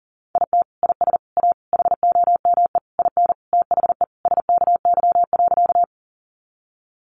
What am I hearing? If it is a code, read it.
Morse code, heard as a keyed tone, about 40 wpm, 700 Hz. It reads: IT IS A HOME IN THE SKY.
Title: Sentences from Top 1200 Words 40wpm